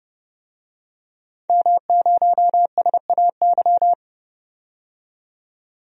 Morse code, heard as a keyed tone, about 30 wpm, 700 Hz. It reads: M0SAY
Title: DXCC-100 Call Signs - 5 Characters - 30wpm